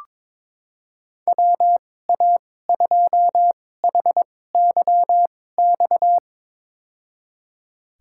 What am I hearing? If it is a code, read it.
Morse code, heard as a keyed tone, about 22 wpm, spaced as for 16 wpm, 700 Hz. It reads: WA2HYX
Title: USA Call Signs - 6 Characters - 2x Character Spacing - 22wpm